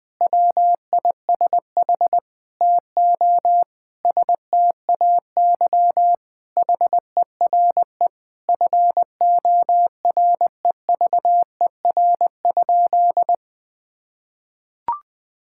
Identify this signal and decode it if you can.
Morse code, heard as a keyed tone, about 20 wpm, 700 Hz. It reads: WISH TO STAY HERE FOREVER?
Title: Sentences from Top 1600 Words 20wpm